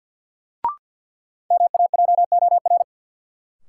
Morse code, heard as a keyed tone, about 51 wpm, 700 Hz. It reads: MAJOR
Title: Sentences from Top 600 Words - New Word Drill - 50wpm